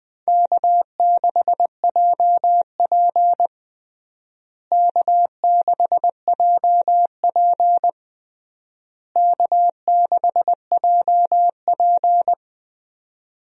Morse code, heard as a keyed tone, about 20 wpm, 700 Hz. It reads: K6JP K6JP K6JP
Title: USA Call Signs - 4 Characters - Repeated 3x - 20wpm